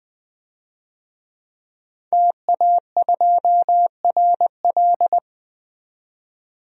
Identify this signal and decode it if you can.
Morse code, heard as a keyed tone, about 20 wpm, 700 Hz. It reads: TA2RL
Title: DXCC-100 Call Signs - 5 Characters - 20wpm